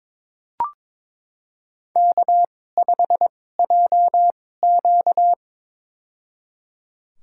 Morse code, heard as a keyed tone, about 22 wpm, spaced as for 16 wpm, 700 Hz. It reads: K5JQ
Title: USA Call Signs - 4 Characters - 2x Character Spacing - 22wpm